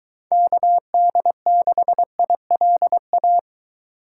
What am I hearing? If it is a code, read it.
Morse code, heard as a keyed tone, about 23 wpm, 700 Hz. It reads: KD6ILA